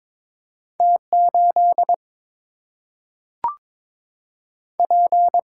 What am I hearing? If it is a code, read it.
Morse code, heard as a keyed tone, about 22 wpm, 700 Hz. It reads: T 8 P
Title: Single-Letter Single-Number 22wpm